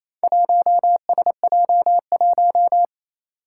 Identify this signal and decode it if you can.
Morse code, heard as a keyed tone, about 28 wpm, 700 Hz. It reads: 1SJ1
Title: Four Challenging Letter-Number Combinations 28wpm